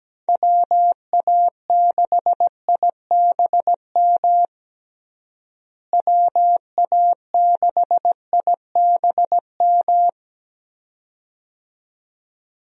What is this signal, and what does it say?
Morse code, heard as a keyed tone, about 17 wpm, 700 Hz. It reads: WA6IBM WA6IBM
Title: USA Call Signs - 6 Characters - Repeated 2x - 17wpm